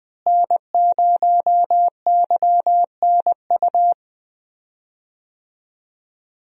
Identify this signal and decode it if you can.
Morse code, heard as a keyed tone, about 20 wpm, 700 Hz. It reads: N0YNU
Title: USA Call Signs - 5 Characters - Repeated 2x - 20wpm